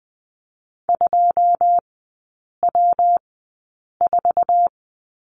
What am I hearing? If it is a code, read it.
Morse code, heard as a keyed tone, about 20 wpm, spaced as for 8 wpm, 700 Hz. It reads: 2W4
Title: 166 Lesson 27e - Number 7 - Three Characters 20wpm